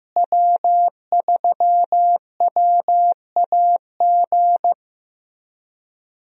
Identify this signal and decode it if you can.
Morse code, heard as a keyed tone, about 15 wpm, 700 Hz. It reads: W3WAG
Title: USA Call Signs - 5 Characters - Repeated 2x - 15wpm